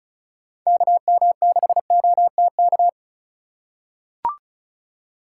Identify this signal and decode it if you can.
Morse code, heard as a keyed tone, about 34 wpm, 700 Hz. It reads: KM6OTK
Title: USA Call Signs - 6 Characters - 4x Character Spacing - 35wpm